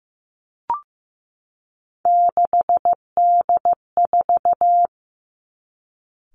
Morse code, heard as a keyed tone, about 15 wpm, 700 Hz. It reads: 6D4
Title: Three Challenging Letter-Number Combinations 15wpm